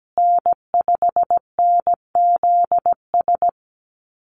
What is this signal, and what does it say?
Morse code, heard as a keyed tone, about 17 wpm, 700 Hz. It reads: N5NZS
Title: USA Call Signs Repeated 3x 17wpm